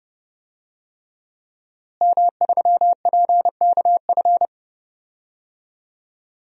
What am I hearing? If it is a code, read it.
Morse code, heard as a keyed tone, about 30 wpm, 700 Hz. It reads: M3PKF
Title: DXCC-100 Call Signs - 5 Characters - Repeated 3x - 30wpm